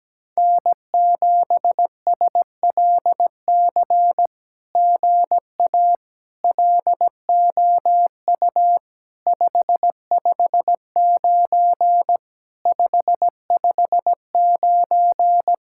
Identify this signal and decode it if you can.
Morse code, heard as a keyed tone, about 17 wpm, 700 Hz. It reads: N7SLC GA LOU 559 559
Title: SOTA QSO Practice 17wpm